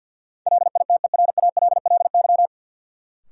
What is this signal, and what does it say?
Morse code, heard as a keyed tone, about 51 wpm, 700 Hz. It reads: LITERALLY
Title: Sentences from Top 1400 Words - New Word Drill - 50wpm